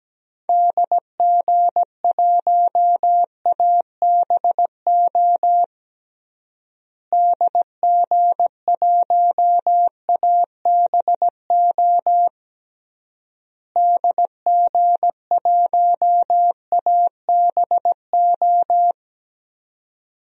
Morse code, heard as a keyed tone, about 17 wpm, 700 Hz. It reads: DG1ABO DG1ABO DG1ABO
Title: DXCC-100 Call Signs - 6 Characters - Repeated 3x - 17wpm